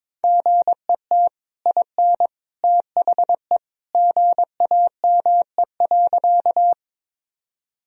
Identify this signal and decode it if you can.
Morse code, heard as a keyed tone, about 22 wpm, 700 Hz. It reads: GET IN THE GAME.
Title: Sentences from Top 1000 Words - Review Part 3 - 22wpm